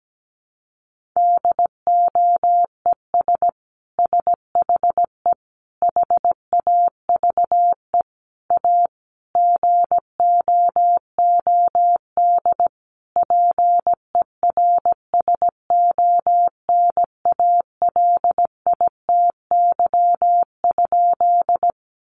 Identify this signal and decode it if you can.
Morse code, heard as a keyed tone, about 17 wpm, 700 Hz. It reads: DOES SHE HAVE A GOOD PERSONALITY?
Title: Sentences from Top 1400 Words 17wpm